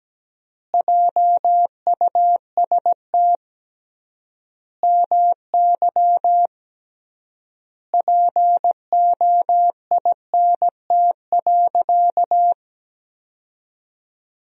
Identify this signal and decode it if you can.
Morse code, heard as a keyed tone, about 17 wpm, 700 Hz. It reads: JUST MY POINT.
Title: 3x Word-Spacing - Sentences from Top 200 Words - 17wpm